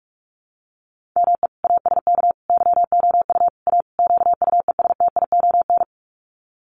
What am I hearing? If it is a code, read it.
Morse code, heard as a keyed tone, about 44 wpm, 690 Hz. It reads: ME ASK YOU A QUESTION
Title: Sets of 5 Words Encore2 45wpm